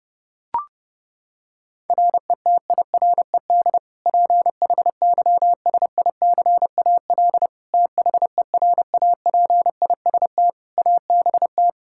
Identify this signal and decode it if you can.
Morse code, heard as a keyed tone, about 30 wpm, 700 Hz. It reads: RETIRED PHYSICAL THERAPIST ABT
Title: K1USN SST QSO Practice 30wpm